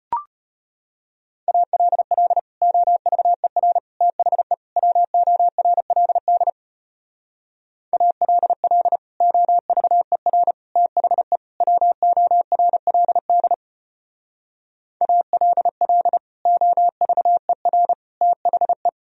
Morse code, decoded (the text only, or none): ALL OVER THE WORLD ALL OVER THE WORLD ALL OVER THE